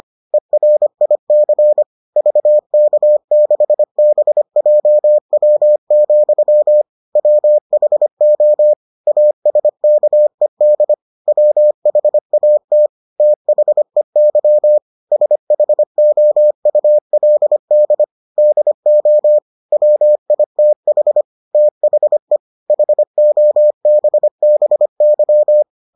Morse code, heard as a keyed tone, about 25 wpm, 600 Hz. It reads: ERIC VK6BJW, WHO ASKED WHAT THEY SHOULD DO WITH THE HOBBY